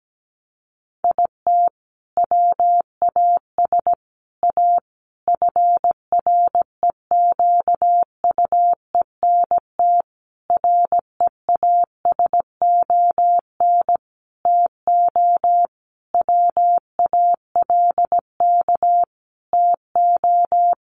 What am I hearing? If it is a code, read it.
Morse code, heard as a keyed tone, about 17 wpm, 700 Hz. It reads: IT WAS A FREQUENT REASON TO WALK TO
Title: Sentences from Top 1400 Words 17wpm